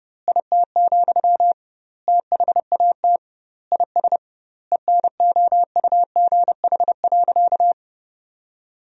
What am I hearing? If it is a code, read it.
Morse code, heard as a keyed tone, about 30 wpm, 700 Hz. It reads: IT, THAT IS ENOUGH.
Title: 2x Word-Spacing - Sentences from Top 400 Words - 30wpm